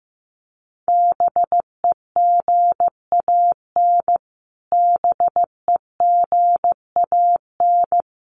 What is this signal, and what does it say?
Morse code, heard as a keyed tone, about 15 wpm, 700 Hz. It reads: BEGAN BEGAN
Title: Top 300 Words Repeated 15wpm